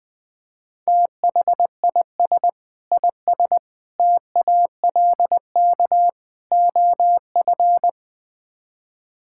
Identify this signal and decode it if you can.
Morse code, heard as a keyed tone, about 20 wpm, 700 Hz. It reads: THIS IS TALK OF
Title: Build A Phrase - 5-Word Phrases - 20wpm